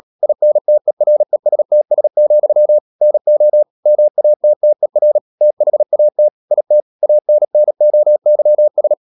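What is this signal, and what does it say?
Morse code, heard as a keyed tone, about 37 wpm, 600 Hz. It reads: INTERESTS, NO MATTER THAT IT ANNOYS
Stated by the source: Morse code practice at 35–39 wpm sending English text of Ken Follett's novel Never, from CW med Gnister.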